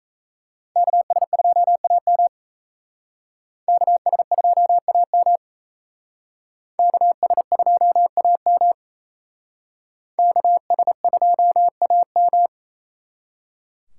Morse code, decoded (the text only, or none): KS2AM KS2AM KS2AM KS2AM